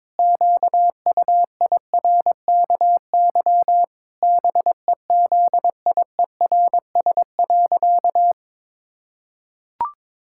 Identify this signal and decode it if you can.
Morse code, heard as a keyed tone, about 22 wpm, 700 Hz. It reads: QUIRKY BEZIERS.
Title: Pangram Challenge v2 - 22wpm